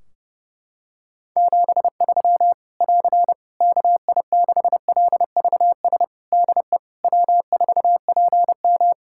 Morse code, heard as a keyed tone, about 30 wpm, 700 Hz. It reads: 73 <AR> KI6LVS DE W4PM
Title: Practice QSO - Exchange 4 - Ending - 30wpm